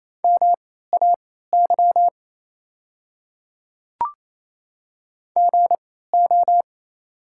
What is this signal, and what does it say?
Morse code, heard as a keyed tone, about 27 wpm, spaced as for 16 wpm, 690 Hz. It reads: MAY GO
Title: 3x Character Spacing - Top 100 Words - 28wpm